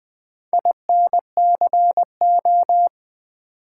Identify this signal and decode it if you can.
Morse code, heard as a keyed tone, about 20 wpm, 700 Hz. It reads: INCO
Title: Build A Word - 2 Letters At A Time - 6-Letter Words - 20wpm